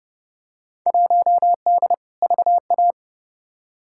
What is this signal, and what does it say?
Morse code, heard as a keyed tone, about 30 wpm, 700 Hz. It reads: 1D VA
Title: Field Day Exchanges - 30wpm